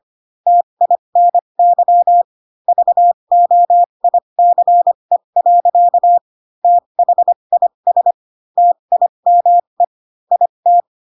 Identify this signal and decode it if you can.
Morse code, heard as a keyed tone, about 25 wpm, spaced as for 22 wpm, 700 Hz. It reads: TINY VOICE. THIS TIME IT